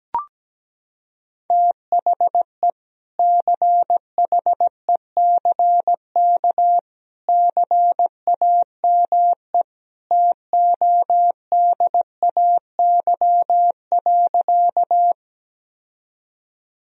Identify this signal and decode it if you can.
Morse code, heard as a keyed tone, about 17 wpm, 700 Hz. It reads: THE CHECK CAME TODAY.
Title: Sentences from Top 1000 Words - Review Part 3 - 17wpm